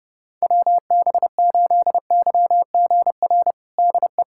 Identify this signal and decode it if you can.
Morse code, heard as a keyed tone, about 30 wpm, 700 Hz. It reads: WB8YGR DE
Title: USA Callsign DE Callsign - 6 Characters - 30wpm